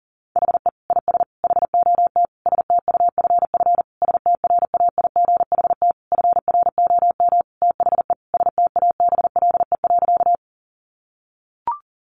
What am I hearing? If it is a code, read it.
Morse code, heard as a keyed tone, about 40 wpm, 700 Hz. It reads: HE IS HOT STUFF STRAIGHT FROM THE STABLE.
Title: Sentences from Top 1500 Words 40wpm